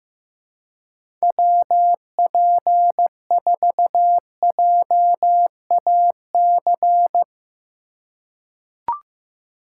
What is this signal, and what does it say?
Morse code, heard as a keyed tone, about 15 wpm, 700 Hz. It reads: WP4JAC
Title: USA Call Signs - 6 Characters - 3x Character Spacing - 15wpm